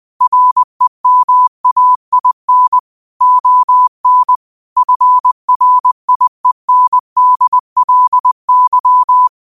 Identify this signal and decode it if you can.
Morse code, heard as a keyed tone, about 20 wpm, 1000 Hz. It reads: REMAIN ON FRIENDLY
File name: qotdfiles / 20wpm